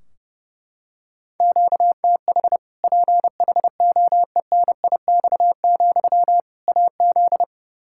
Morse code, heard as a keyed tone, about 28 wpm, 700 Hz. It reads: QTH PHOENIX, AZ
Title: Practice QSO - Exchange 1 - Beginning - 30wpm